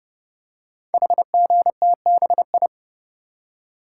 Morse code, heard as a keyed tone, about 30 wpm, 700 Hz. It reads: HGTBI
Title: Code Groups - 5 Letters - 30wpm